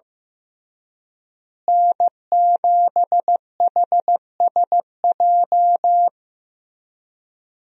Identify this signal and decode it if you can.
Morse code, heard as a keyed tone, about 15 wpm, 700 Hz. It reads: N7HSJ
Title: USA Call Signs - 5 Characters - Repeated 3x - 15wpm